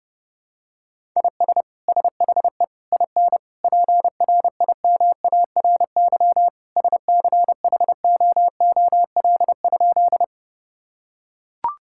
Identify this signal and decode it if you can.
Morse code, heard as a keyed tone, about 30 wpm, 700 Hz. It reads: IS SHE IN PRIMARY SCHOOL?
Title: Sentences from Top 1000 Words - Review Part 5 - 30wpm